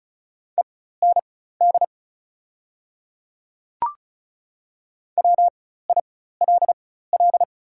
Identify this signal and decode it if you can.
Morse code, heard as a keyed tone, about 36 wpm, spaced as for 16 wpm, 700 Hz. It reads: END WILL
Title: 4x Character Spacing - Top 200 Words - 35wpm